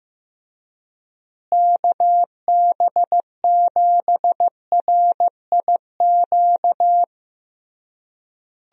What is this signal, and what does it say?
Morse code, heard as a keyed tone, about 15 wpm, 700 Hz. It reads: KB7RIQ
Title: USA Call Signs - 6 Characters - Repeated 3x - 15wpm